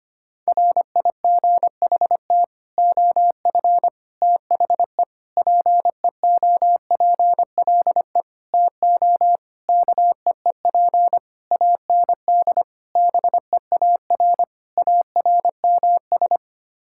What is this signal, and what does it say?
Morse code, heard as a keyed tone, about 25 wpm, 700 Hz. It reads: RIGHT OF THE PEOPLE TO KEEP AND BEAR ARMS
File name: Sets of 9 Words - 25wpm